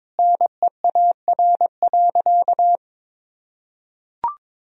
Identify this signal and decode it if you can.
Morse code, heard as a keyed tone, about 22 wpm, 700 Hz. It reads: NEAR.
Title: Sentences from Top 300 Words and Abbreviations 22wpm